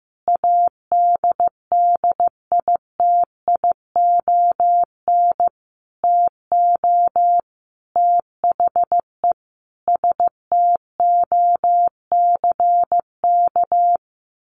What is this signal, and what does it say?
Morse code, heard as a keyed tone, about 15 wpm, 700 Hz. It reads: ADDITION TO THE STOCK
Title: Sentences from Top 1500 Words 15wpm